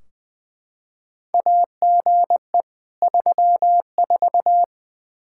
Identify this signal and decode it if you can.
Morse code, heard as a keyed tone, about 20 wpm, 700 Hz. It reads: AGE 34
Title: Practice QSO - Exchange 3 - More Info - 20wpm